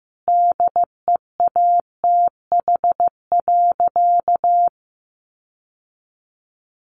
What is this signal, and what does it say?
Morse code, heard as a keyed tone, about 15 wpm, 700 Hz. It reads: DEATH.
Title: Sentences from Top 1100 Words 15wpm